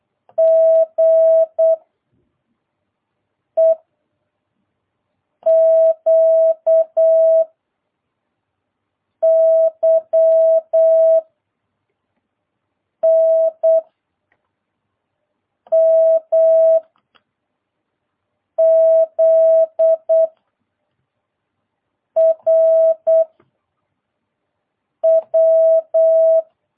Morse code, heard as a keyed tone, about 8 wpm, 660 Hz. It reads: G E Q Y N M Z R W